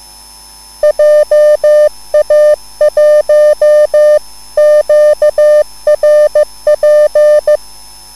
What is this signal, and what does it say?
Morse code, heard as a keyed tone, about 15 wpm, 600 Hz. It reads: JA1QRP
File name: QSL / CW lesson